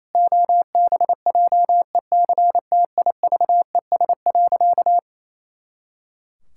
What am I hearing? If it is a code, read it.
Morse code, heard as a keyed tone, about 27 wpm, 700 Hz. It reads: OBJECTIVES.
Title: Pangram Challenge v2 - 28wpm